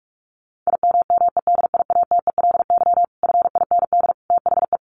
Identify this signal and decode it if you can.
Morse code, heard as a keyed tone, about 44 wpm, 700 Hz. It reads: IMMEDIATELY FIND THE
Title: Sentences from Top 1200 Words 45wpm